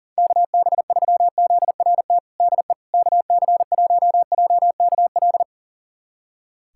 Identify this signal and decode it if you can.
Morse code, heard as a keyed tone, about 37 wpm, 700 Hz. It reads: KB3ZRT DE KC1JKL
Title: USA Callsign DE Callsign 40wpm